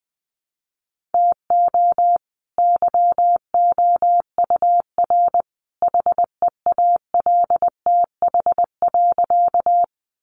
Morse code, heard as a keyed tone, about 20 wpm, 700 Hz. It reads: TO YOUR HEALTH.
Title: Sentences from Top 600 Words 20wpm